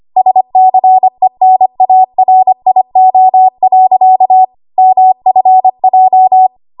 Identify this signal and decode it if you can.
Morse code, heard as a keyed tone, about 25 wpm, 750 Hz. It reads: SCENARIO. MFJ
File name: Morse / 25WPM